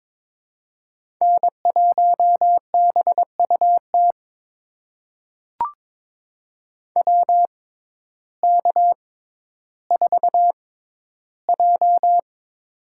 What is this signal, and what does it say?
Morse code, heard as a keyed tone, about 22 wpm, spaced as for 12 wpm, 700 Hz. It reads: N1BUT WK4J
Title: USA Call Signs - 6x Character Spacing - 22wpm